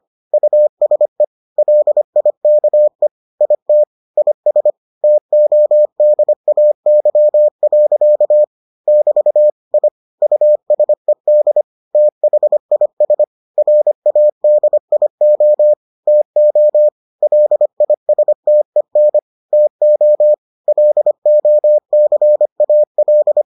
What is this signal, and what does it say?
Morse code, heard as a keyed tone, about 25 wpm, 600 Hz. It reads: USE LIKE IT IS TODAY. <BT> I USED THIS RADIO TO LISTEN TO LOCAL